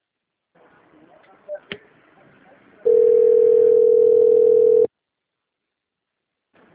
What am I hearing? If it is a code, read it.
Ringback tone (North America), 440 + 480 Hz.